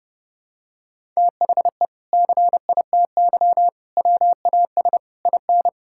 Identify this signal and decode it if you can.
Morse code, heard as a keyed tone, about 30 wpm, 700 Hz. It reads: THE CITY WAS IN
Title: Sentences from Top 1600 Words 30wpm